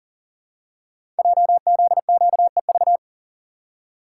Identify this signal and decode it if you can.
Morse code, heard as a keyed tone, about 39 wpm, 700 Hz. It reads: JZQEV
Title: Code Groups - 5 Letters - 40wpm